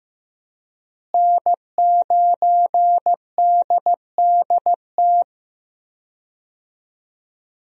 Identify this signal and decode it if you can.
Morse code, heard as a keyed tone, about 15 wpm, 700 Hz. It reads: N9DDT
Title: USA Call Signs - 5 Characters - Repeated 3x - 15wpm